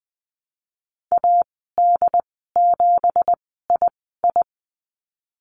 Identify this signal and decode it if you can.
Morse code, heard as a keyed tone, about 20 wpm, spaced as for 14 wpm, 700 Hz. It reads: AD7II
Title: 186 Lesson 30g - Letter b - Call Signs 20wpm